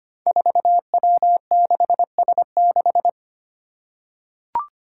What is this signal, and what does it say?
Morse code, heard as a keyed tone, about 25 wpm, 700 Hz. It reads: 4W6S6
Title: Five Challenging Letter-Number Combinations 25wpm